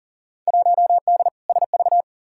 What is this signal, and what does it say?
Morse code, heard as a keyed tone, about 40 wpm, 700 Hz. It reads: 1D SV
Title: Field Day Exchanges - 40wpm